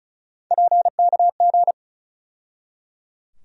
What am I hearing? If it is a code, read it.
Morse code, heard as a keyed tone, about 34 wpm, 700 Hz. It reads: PKG
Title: USA Call Sign Postfixes - 3 Letters 35wpm